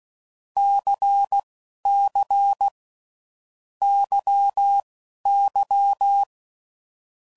Morse code, heard as a keyed tone, about 16 wpm, spaced as for 11 wpm, 800 Hz. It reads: CC YY
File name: Groups / Basic Letter Groups / LFCYQ